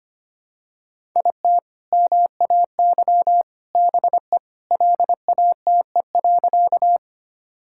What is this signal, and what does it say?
Morse code, heard as a keyed tone, about 25 wpm, 700 Hz. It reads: IT MAY BE LATE.
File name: practice / 25wpm / Sentences with 3-Word Phrases - 25wpm